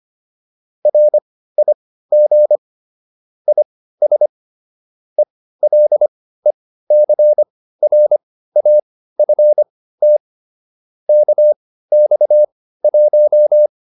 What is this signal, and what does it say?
Morse code, heard as a keyed tone, about 25 wpm, spaced as for 15 wpm, 600 Hz. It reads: RIG IS ELECRAFT KX1